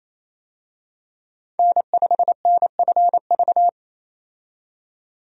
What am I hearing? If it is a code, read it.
Morse code, heard as a keyed tone, about 28 wpm, 700 Hz. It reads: N5NFV
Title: USA Call Signs - 5 Characters - 28wpm